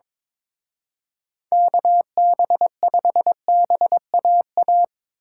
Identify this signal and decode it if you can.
Morse code, heard as a keyed tone, about 24 wpm, 700 Hz. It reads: KB5BAA
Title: Speed Racing - USA 2x3 Call Signs 22wpm